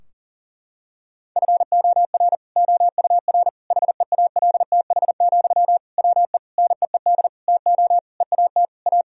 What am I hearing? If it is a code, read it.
Morse code, heard as a keyed tone, about 40 wpm, 700 Hz. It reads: FOR OUR HEALTH, WE NEED TO EAT A
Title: Sentences from Top 1000 Words - Review Part 4 - 40wpm